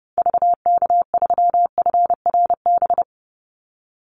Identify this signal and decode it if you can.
Morse code, heard as a keyed tone, about 30 wpm, 700 Hz. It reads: VK3FRB
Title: USA Call Signs 30wpm